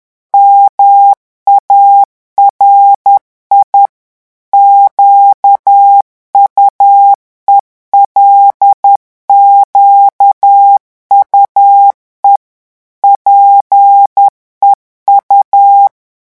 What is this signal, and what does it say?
Morse code, heard as a keyed tone, about 10 wpm, 800 Hz. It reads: MARI QUELQUE PEU
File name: morse mp3 / revision9 citations humoristiques vitesse 10 mots minute